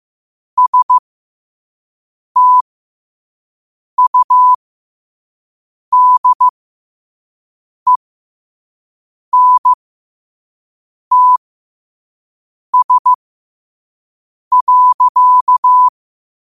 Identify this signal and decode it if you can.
Morse code, heard as a keyed tone, about 15 wpm, spaced as for 5 wpm, 1000 Hz. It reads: STUDENTS.